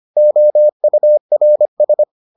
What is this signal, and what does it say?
Morse code, heard as a keyed tone, about 25 wpm, 600 Hz. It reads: OURS